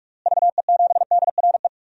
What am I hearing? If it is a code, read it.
Morse code, heard as a keyed tone, about 42 wpm, 700 Hz. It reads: VE7DRE
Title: DXCC-100 Call Signs - 6 Characters - Repeated 2x - 45wpm